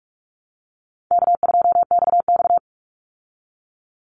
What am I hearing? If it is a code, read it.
Morse code, heard as a keyed tone, about 44 wpm, 700 Hz. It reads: K2XX
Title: Call Signs Repeated 2x - 45wpm